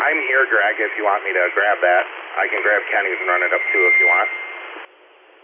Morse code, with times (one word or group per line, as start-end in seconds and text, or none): none